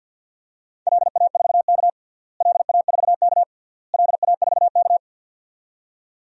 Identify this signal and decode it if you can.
Morse code, heard as a keyed tone, about 48 wpm, 700 Hz. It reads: LA4K LA4K LA4K
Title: DXCC-100 Call Signs - 4 Characters - Repeated 3x - 50wpm